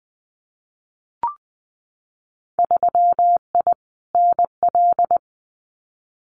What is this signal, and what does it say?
Morse code, heard as a keyed tone, about 20 wpm, 700 Hz. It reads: 3I NL
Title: Winter Field Day Exchanges 20wpm